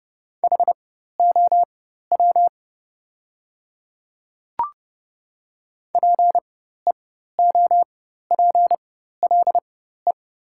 Morse code, heard as a keyed tone, about 30 wpm, spaced as for 14 wpm, 700 Hz. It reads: HOW PEOPLE
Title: 4x Character Spacing - Top 100 Words - 30wpm